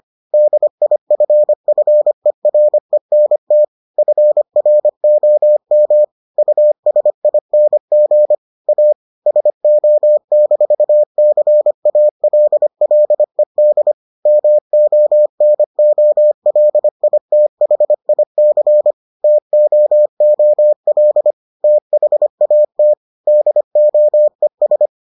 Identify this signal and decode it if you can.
Morse code, heard as a keyed tone, about 25 wpm, 600 Hz. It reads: DIFFERENT FROM USING A SO-CALLED MONOLITHIC TOOL THAT DOES